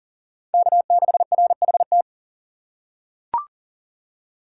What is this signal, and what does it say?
Morse code, heard as a keyed tone, about 40 wpm, 700 Hz. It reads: K6RHT
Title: USA Call Signs - 2x Character Spacing - 40wpm